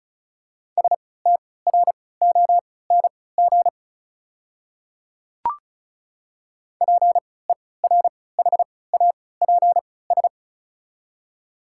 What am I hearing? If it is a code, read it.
Morse code, heard as a keyed tone, about 34 wpm, spaced as for 20 wpm, 690 Hz. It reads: STRONG PERHAPS
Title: 3x Character Spacing - Top 500 Words - 35wpm